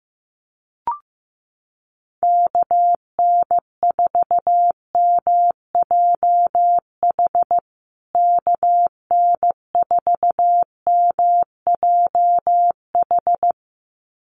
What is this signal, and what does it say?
Morse code, heard as a keyed tone, about 15 wpm, 700 Hz. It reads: KN4MJH KN4MJH
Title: USA Call Signs Repeated 2x 15wpm